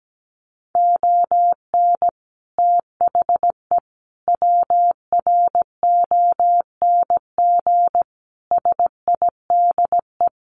Morse code, heard as a keyed tone, about 17 wpm, 700 Hz. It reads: ON THE WRONG SIDE
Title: Sets of 5 Words Encore 17wpm